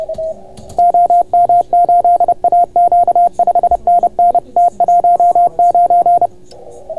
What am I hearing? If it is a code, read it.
Morse code, heard as a keyed tone, about 30 wpm, 690 Hz. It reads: OM8AQ 5NN T19